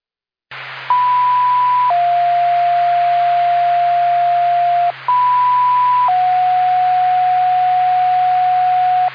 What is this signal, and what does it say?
Two-tone page, heard as a one-second tone, then a three-second tone, about 990 then 700 Hz, about 990 then 730 Hz. It is heard from 0.9 to 9.1 s.